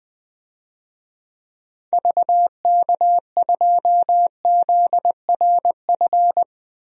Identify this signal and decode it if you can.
Morse code, heard as a keyed tone, about 20 wpm, 700 Hz. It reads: VK2ZRF
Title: DXCC-100 Call Signs - 6 Characters - Repeated 2x - 20wpm